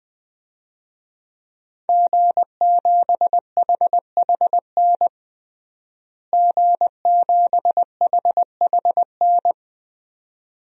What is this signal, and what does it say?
Morse code, heard as a keyed tone, about 20 wpm, 700 Hz. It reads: G7HHN G7HHN
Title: DXCC-100 Call Signs - 5 Characters - Repeated 2x - 20wpm